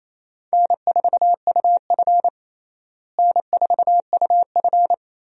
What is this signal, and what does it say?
Morse code, heard as a keyed tone, about 27 wpm, 700 Hz. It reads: N4UF N4UF
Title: USA Call Signs - 4 Characters - Repeated 2x - 28wpm